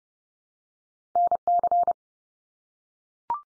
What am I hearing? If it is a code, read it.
Morse code, heard as a keyed tone, about 30 wpm, 700 Hz. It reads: NC
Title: US States 30WPM